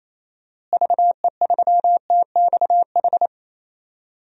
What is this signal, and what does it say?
Morse code, heard as a keyed tone, about 28 wpm, 700 Hz. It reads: VE3TXH
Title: DXCC-100 Call Signs - 6 Characters - Repeated 3x - 28wpm